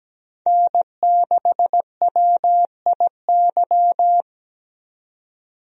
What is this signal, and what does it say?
Morse code, heard as a keyed tone, about 17 wpm, 700 Hz. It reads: N6WIY
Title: USA Call Signs - 5 Characters - 17wpm